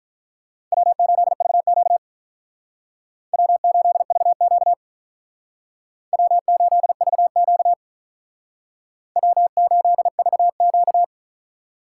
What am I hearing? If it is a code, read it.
Morse code, heard as a keyed tone, about 45 wpm, 700 Hz. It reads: W8VQ W8VQ W8VQ W8VQ